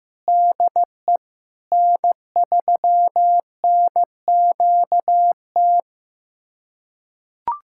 Morse code, heard as a keyed tone, about 15 wpm, 700 Hz. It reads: DE N3NQT
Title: USA Callsign DE Callsign - 5 Characters - 15wpm